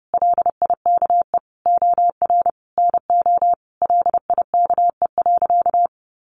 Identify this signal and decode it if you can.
Morse code, heard as a keyed tone, about 30 wpm, 700 Hz. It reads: LIKE OR NO LIKE.
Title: Sentences from Top 1000 Words 30wpm